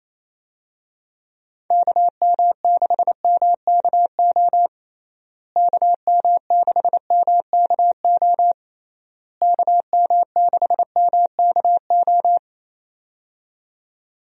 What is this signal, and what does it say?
Morse code, heard as a keyed tone, about 28 wpm, 700 Hz. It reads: KM6MKO KM6MKO KM6MKO
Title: USA Call Signs - 6 Characters - Repeated 3x - 28wpm